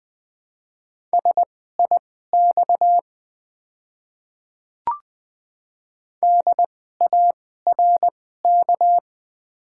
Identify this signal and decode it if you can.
Morse code, heard as a keyed tone, about 20 wpm, spaced as for 15 wpm, 690 Hz. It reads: SIX DARK
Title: 2x Character Spacing - Top 500 Words - 20wpm